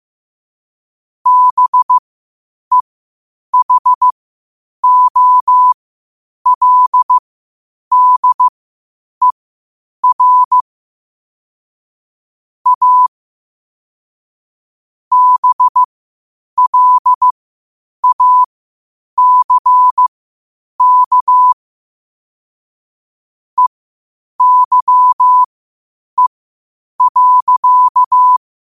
Morse code, heard as a keyed tone, about 15 wpm, spaced as for 8 wpm, 1000 Hz. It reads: BEHOLDER A BLACK EYE.
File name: qotdfiles / 8wpm